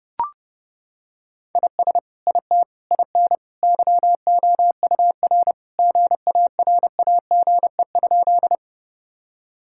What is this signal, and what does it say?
Morse code, heard as a keyed tone, about 30 wpm, 700 Hz. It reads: IS IT IN YOUR GARAGE?
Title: Sentences from Top 1600 Words 30wpm